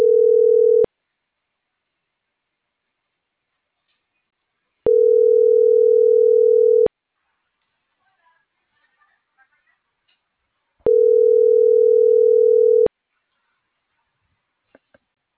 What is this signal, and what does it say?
Ringback tone (North America), 440 + 480 Hz.